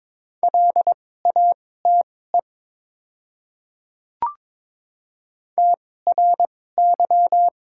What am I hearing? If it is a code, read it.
Morse code, heard as a keyed tone, about 22 wpm, spaced as for 16 wpm, 700 Hz. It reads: LATE TRY
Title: 2x Character Spacing - Top 300 Words - 22wpm